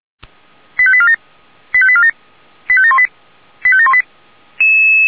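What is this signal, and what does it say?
Five-tone selective call (ZVEI), heard as quick runs of five tones, tones of about 70 ms. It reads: 85847 85847 85718 85718
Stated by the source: Five-tone ZVEI selective call received on SDR.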